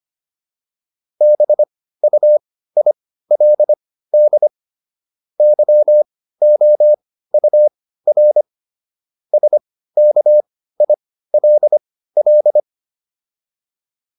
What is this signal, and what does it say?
Morse code, heard as a keyed tone, about 25 wpm, spaced as for 15 wpm, 600 Hz. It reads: BUILD YOUR SKILL